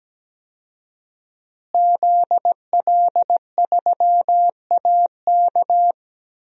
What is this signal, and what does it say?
Morse code, heard as a keyed tone, about 17 wpm, 700 Hz. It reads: ZL3AK
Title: DXCC-100 Call Signs - 5 Characters - Repeated 2x - 17wpm